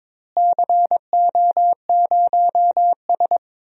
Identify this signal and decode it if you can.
Morse code, heard as a keyed tone, about 22 wpm, 700 Hz. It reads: CO0S
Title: DXCC-100 Call Signs - 4 Characters - Repeated 3x - 22wpm